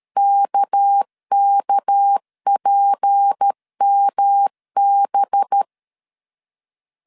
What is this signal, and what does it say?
Morse code, heard as a keyed tone, about 13 wpm, 800 Hz. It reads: KKPMB
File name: CW / DET / 1 6 1500C S B 12wpm 3431